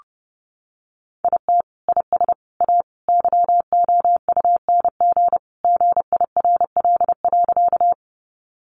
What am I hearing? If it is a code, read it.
Morse code, heard as a keyed tone, about 28 wpm, 700 Hz. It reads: IT IS A YOUNG GIRL.
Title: Sentences from Top 400 Words 30wpm